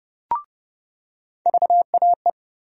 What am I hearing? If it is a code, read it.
Morse code, heard as a keyed tone, about 32 wpm, 700 Hz. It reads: VAE
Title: USA Call Sign Postfixes - 3 Letters 30wpm